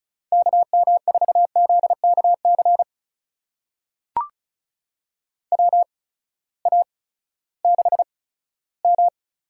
Morse code, heard as a keyed tone, about 36 wpm, spaced as for 18 wpm, 700 Hz. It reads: KM4ZKC WA6M
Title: USA Call Signs - 6 Characters - 8x Character Spacing - 35wpm